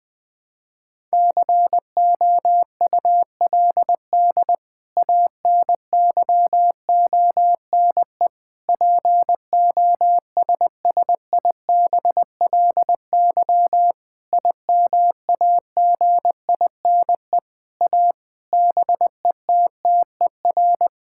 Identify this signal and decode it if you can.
Morse code, heard as a keyed tone, about 21 wpm, 700 Hz. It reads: COULD ANYONE POSSIBLY IMAGINE A BETTER
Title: Sentences from Top 1600 Words 20wpm